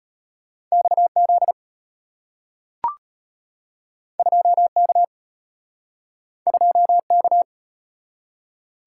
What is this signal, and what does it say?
Morse code, heard as a keyed tone, about 33 wpm, 690 Hz. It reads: XZ 2K 2K